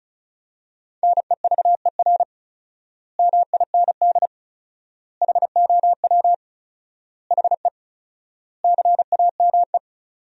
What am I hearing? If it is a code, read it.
Morse code, heard as a keyed tone, about 34 wpm, 700 Hz. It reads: NEVER MIND HOW HE CAME
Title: 4x Word-Spacing - Sentences from Top 500 Words - 35wpm